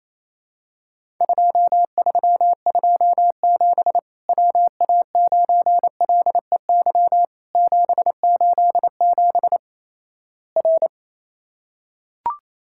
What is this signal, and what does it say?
Morse code, heard as a keyed tone, about 28 wpm, 700 Hz. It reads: 2327 WA9LEY 787 R
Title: Worked All Europe DX Contest - QTC Exchange - 28wpm